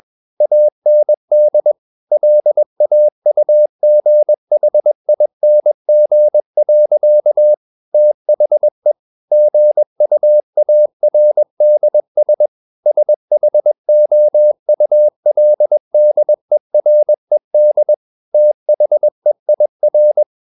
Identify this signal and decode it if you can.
Morse code, heard as a keyed tone, about 21 wpm, 600 Hz. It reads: AND LAUGHING. THE GUARDS SHOULDERED THEIR
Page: CW med Gnister